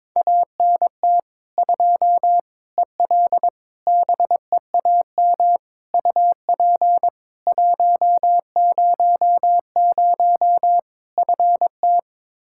Morse code, heard as a keyed tone, about 21 wpm, 700 Hz. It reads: ANT 2 EL BEAM UP 100 FT
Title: Practice QSO - Exchange 2 - Info - 22wpm